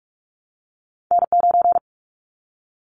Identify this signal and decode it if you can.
Morse code, heard as a keyed tone, about 46 wpm, 700 Hz. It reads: N9
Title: Callsign Prefixes for Top 25 DXCC - 45wpm